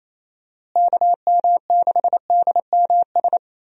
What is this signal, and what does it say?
Morse code, heard as a keyed tone, about 28 wpm, 700 Hz. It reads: KM6DMS
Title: USA Call Signs - 6 Characters - Repeated 3x - 28wpm